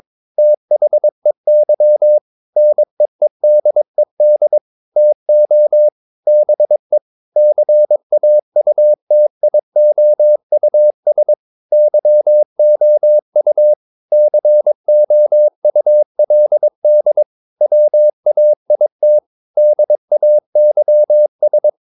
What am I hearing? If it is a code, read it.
Morse code, heard as a keyed tone, about 22 wpm, 600 Hz. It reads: THEY NEEDED TO BE CAUTIOUS YOU COULD WAIT DAYS
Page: Romanen Never af Ken Follet på engelsk.